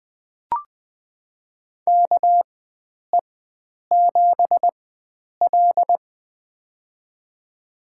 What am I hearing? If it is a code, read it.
Morse code, heard as a keyed tone, about 20 wpm, spaced as for 9 wpm, 700 Hz. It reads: KE7L
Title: USA Call Signs - 4 Characters - 4x Character Spacing - 20wpm